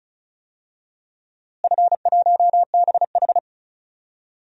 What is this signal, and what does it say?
Morse code, heard as a keyed tone, about 36 wpm, 700 Hz. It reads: F1BH
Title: DXCC-100 Call Signs - 4 Characters - Repeated 2x - 35wpm